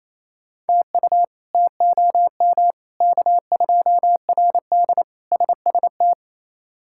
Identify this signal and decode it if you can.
Morse code, heard as a keyed tone, about 28 wpm, 700 Hz. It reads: TU TOM K2RD SST
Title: K1USN SST QSO Practice - 28wpm